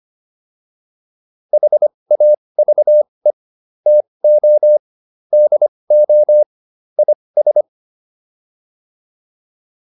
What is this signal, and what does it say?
Morse code, heard as a keyed tone, about 25 wpm, spaced as for 20 wpm, 600 Hz. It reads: HAVE TO DO IS